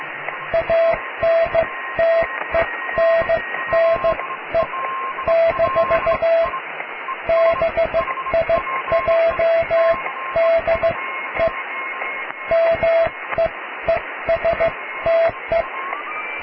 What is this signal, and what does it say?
Morse code, heard as a keyed tone, about 15 wpm, spaced as for 13 wpm, 660 Hz, with radio noise. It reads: ANTENNE - BIJ DE MEESTE
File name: cw / tekst(a)+QRM